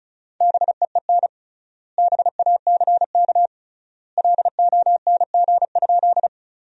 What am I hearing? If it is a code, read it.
Morse code, heard as a keyed tone, about 34 wpm, 700 Hz. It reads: BEEN BACK LONG?
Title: 3x Word-Spacing - Sentences from Top 200 Words - 35wpm